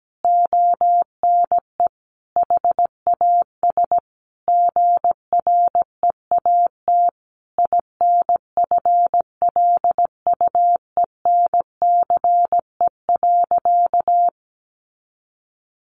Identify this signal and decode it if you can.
Morse code, heard as a keyed tone, about 17 wpm, 700 Hz. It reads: ONE HAS GREAT INFLUENCE.
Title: Sentences from Top 1100 Words 17wpm